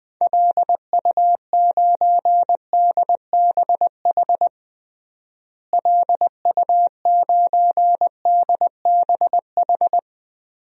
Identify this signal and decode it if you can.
Morse code, heard as a keyed tone, about 20 wpm, 700 Hz. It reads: LU9DBH LU9DBH
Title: DXCC-100 Call Signs - 6 Characters - Repeated 2x - 20wpm